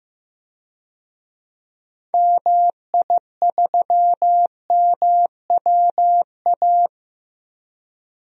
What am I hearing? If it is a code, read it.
Morse code, heard as a keyed tone, about 15 wpm, 700 Hz. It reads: MI3MWA
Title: DXCC-100 Call Signs - 6 Characters - Repeated 3x - 15wpm